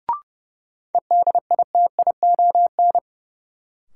Morse code, heard as a keyed tone, about 30 wpm, 700 Hz. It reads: EDITION
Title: Sets of 5 Words - New Word Drill - 30wpm